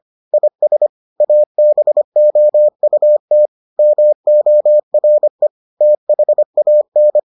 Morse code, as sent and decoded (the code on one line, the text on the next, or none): .. ... / .- -... --- ..- - / -- --- .-. / - .... .- -.
IS ABOUT MORE THAN